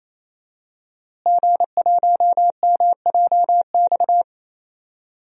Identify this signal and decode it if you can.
Morse code, heard as a keyed tone, about 27 wpm, 700 Hz. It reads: G1MJX
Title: DXCC-100 Call Signs - 5 Characters - Repeated 3x - 28wpm